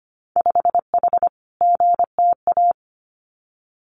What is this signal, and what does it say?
Morse code, heard as a keyed tone, about 25 wpm, 700 Hz. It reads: 5H GTA
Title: Winter Field Day Exchanges Encore 25wpm